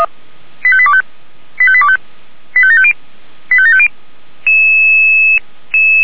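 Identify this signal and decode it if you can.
Five-tone selective call (ZVEI), heard as quick runs of five tones, tones of about 70 ms. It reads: 85825 85825 85750 85750
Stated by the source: ZVEI five-tone selective call received by SDR.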